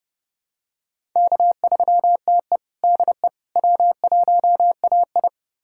Morse code, heard as a keyed tone, about 30 wpm, 700 Hz. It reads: K3TE DE W1AI
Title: USA Callsign DE Callsign - 4 Characters - 30wpm